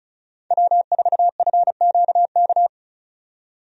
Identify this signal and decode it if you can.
Morse code, heard as a keyed tone, about 36 wpm, 700 Hz. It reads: W4FQK
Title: USA Call Signs - 5 Characters - 35wpm